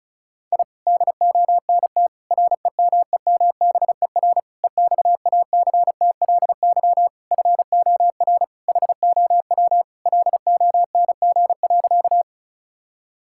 Morse code, heard as a keyed tone, about 34 wpm, 700 Hz. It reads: I DONT REMEMBER EXACTLY FOR HOW LONG.
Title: Sentences from Top 1000 Words - Review Part 6 - 35wpm